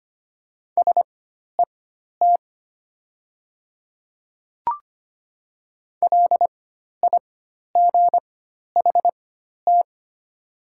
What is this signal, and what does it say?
Morse code, heard as a keyed tone, about 25 wpm, spaced as for 12 wpm, 700 Hz. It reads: SET LIGHT
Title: 4x Character Spacing - Top 300 Words - 25wpm